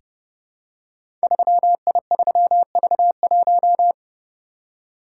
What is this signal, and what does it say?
Morse code, heard as a keyed tone, about 30 wpm, 700 Hz. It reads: 3I3V1
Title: Code Groups - 5 Letter:Numbers - 30wpm